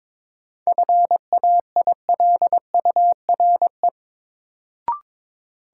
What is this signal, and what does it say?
Morse code, heard as a keyed tone, about 22 wpm, 700 Hz. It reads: FAILURE
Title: Sentences from Top 900 Words - New Word Drill - 22wpm